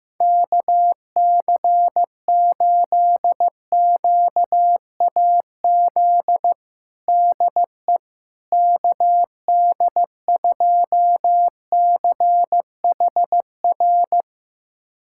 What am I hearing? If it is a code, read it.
Morse code, heard as a keyed tone, about 16 wpm, 700 Hz. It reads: KC8QAZ DE KD2CHR
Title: USA Callsign DE Callsign - 6 Characters - 15wpm